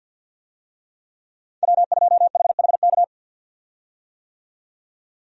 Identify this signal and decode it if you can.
Morse code, heard as a keyed tone, about 50 wpm, 700 Hz. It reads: W2HHK
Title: USA Call Signs - 5 Characters - 50wpm